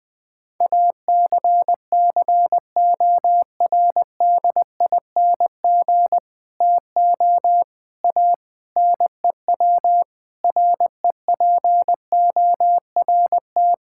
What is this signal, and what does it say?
Morse code, heard as a keyed tone, about 19 wpm, 700 Hz. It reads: ACCORDING TO A NEW REPORT
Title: Sentences with 7-Word Phrases - Spoken Beginning - 20wpm